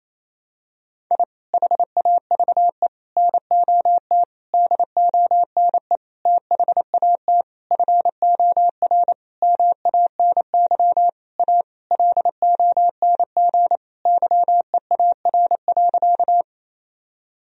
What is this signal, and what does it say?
Morse code, heard as a keyed tone, about 28 wpm, 700 Hz. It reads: I HAVE NOT DONE THAT FOR MANY A LONG YEAR.
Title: Sentences from Top 1000 Words - Review Part 1 - 28wpm